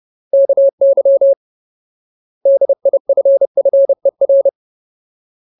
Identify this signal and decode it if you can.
Morse code, heard as a keyed tone, about 30 wpm, 550 Hz. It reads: KY DIFFER
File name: Ultimate Word List / 30WPM Ultimate